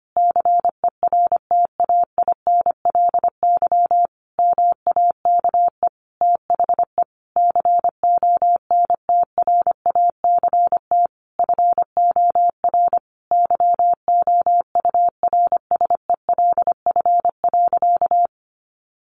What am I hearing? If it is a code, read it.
Morse code, heard as a keyed tone, about 25 wpm, 700 Hz. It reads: CERTAINLY MAKE THE CONTRACT FOR YOURSELF.
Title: Sentences from Top 1400 Words 25wpm